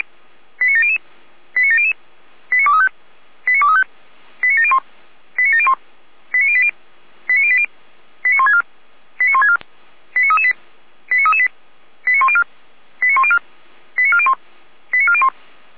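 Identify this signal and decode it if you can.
Five-tone selective call (ZVEI), heard as quick runs of five tones, tones of about 70 ms. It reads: 89700 89700 89235 89235 89701 89701 89080 89080 89164 89164 89338 89338 89104 89104 89401 89401
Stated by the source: ZVEI five-tone call received on SDR.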